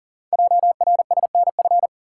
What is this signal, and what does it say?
Morse code, heard as a keyed tone, about 42 wpm, 700 Hz. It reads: JRSNF
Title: Code Groups - 5 Letters - 40wpm